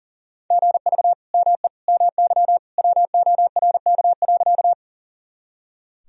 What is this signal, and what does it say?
Morse code, heard as a keyed tone, about 40 wpm, 700 Hz. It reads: GV ME MY WORK.
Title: Sentences from Top 200 Words and Abbreviations 40wpm